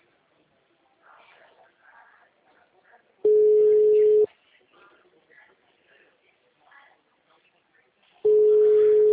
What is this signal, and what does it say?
Ringback tone (Europe, 425 Hz), with speech.